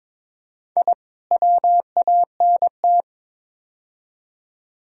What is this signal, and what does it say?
Morse code, heard as a keyed tone, about 22 wpm, 700 Hz. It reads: I WANT
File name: practice / 22wpm / Build A Phrase - 5-Word Phrases - 22wpm